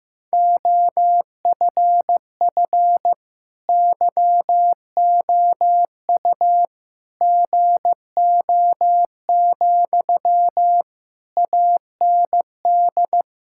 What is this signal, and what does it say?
Morse code, heard as a keyed tone, about 15 wpm, 700 Hz. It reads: OFF YOU GO, AND
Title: Sentences from Top 1000 Words - Review Part 6 - 15wpm